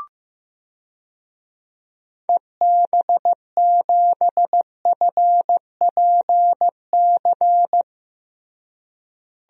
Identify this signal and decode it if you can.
Morse code, heard as a keyed tone, about 15 wpm, 700 Hz. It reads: EB7FPC
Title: DXCC-100 Call Signs - 6 Characters - Repeated 3x - 15wpm